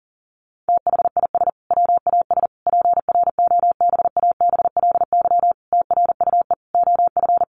Morse code, heard as a keyed tone, about 40 wpm, 700 Hz. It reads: THIS WAS PROBABLY TRUE OF
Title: Sentences from Top 700 Words 40wpm